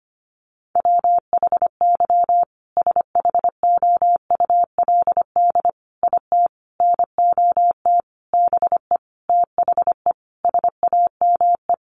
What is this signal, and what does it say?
Morse code, heard as a keyed tone, about 25 wpm, 700 Hz. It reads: WHY SHOULD IT NOT BE THE SAME
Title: Sentences from Top 1400 Words 25wpm